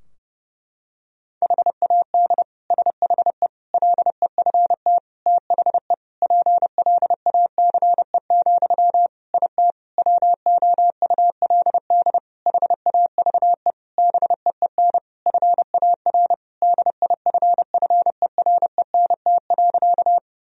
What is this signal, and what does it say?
Morse code, heard as a keyed tone, about 30 wpm, 700 Hz. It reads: HAD SHE LEFT THE PLACE, IT WOULD HAVE BEEN FAR DIFFERENT.
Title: Sentences from Top 1000 Words - Review Part 3 - 30wpm